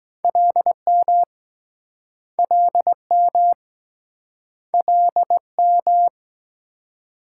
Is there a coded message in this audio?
Yes: Morse code from 0.2 to 6.1 s, heard as a keyed tone, about 24 wpm, 700 Hz. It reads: LM LM LM